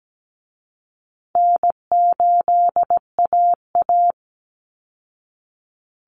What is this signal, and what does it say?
Morse code, heard as a keyed tone, about 17 wpm, 700 Hz. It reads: N8AA
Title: Call Signs Repeated 2x - 17wpm